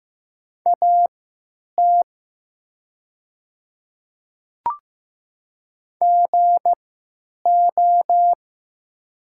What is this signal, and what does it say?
Morse code, heard as a keyed tone, about 15 wpm, spaced as for 9 wpm, 690 Hz. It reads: AT GO
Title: 3x Character Spacing - Top 100 Words - 15wpm